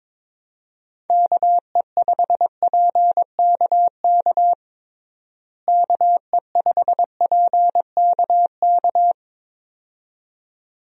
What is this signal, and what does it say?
Morse code, heard as a keyed tone, about 22 wpm, 700 Hz. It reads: KE5PKK KE5PKK
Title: USA Call Signs - 6 Characters - Repeated 2x - 22wpm